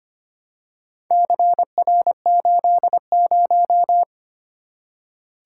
Morse code, heard as a keyed tone, about 25 wpm, 700 Hz. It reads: CR80
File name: practice / 25wpm / Four Letter-Number Combinations - 25wpm